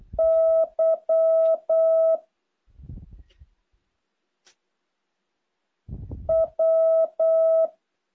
Morse code, heard as a keyed tone, about 8 wpm, 650 Hz. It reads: Y W
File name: jh8cbh / 2023cw